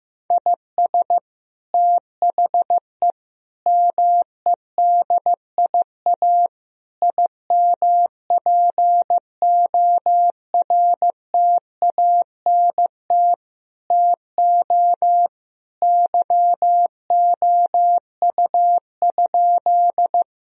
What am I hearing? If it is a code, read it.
Morse code, heard as a keyed tone, about 15 wpm, 700 Hz. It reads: IS THE MEDIA IMPORTANT TO YOU?
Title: Sentences from Top 1000 Words - Review Part 4 - 15wpm